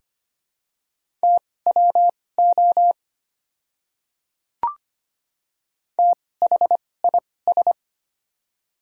Morse code, heard as a keyed tone, about 25 wpm, spaced as for 18 wpm, 700 Hz. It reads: TWO THIS
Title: 2x Character Spacing - Top 100 Words - 25wpm